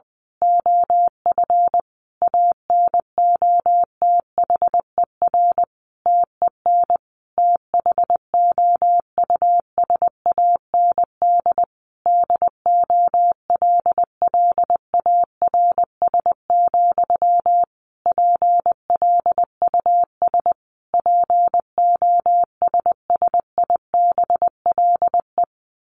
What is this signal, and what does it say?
Morse code, heard as a keyed tone, about 20 wpm, 700 Hz. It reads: OF ANOTHER TEN THOUSAND DOLLARS, PLUS POSSIBLE
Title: A Sound Of Thunder 20wpm